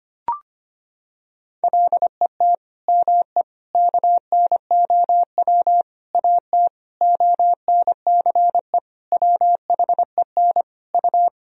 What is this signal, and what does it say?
Morse code, heard as a keyed tone, about 25 wpm, 700 Hz. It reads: LET ME KNOW AT ONCE WHEN U
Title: Sentences from Top 300 Words and Abbreviations 25wpm